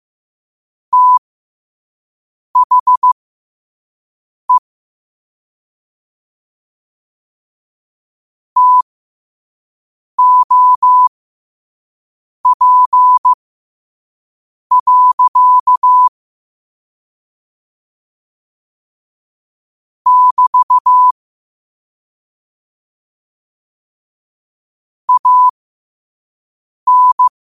Morse code, heard as a keyed tone, about 15 wpm, spaced as for 5 wpm, 1000 Hz. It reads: THE TOP. <BT> AN
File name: qotdfiles / 5wpm